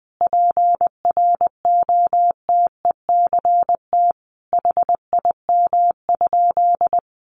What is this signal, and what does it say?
Morse code, heard as a keyed tone, about 20 wpm, 700 Hz. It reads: PROTECT HIM?
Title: Sentences from Top 900 Words 20wpm